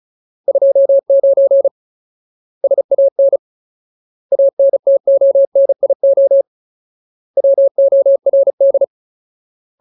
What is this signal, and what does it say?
Morse code, heard as a keyed tone, about 34 wpm, 550 Hz. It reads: 29 SAN ANTONIO WORD
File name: Ultimate Word List / 35WPM Ultimate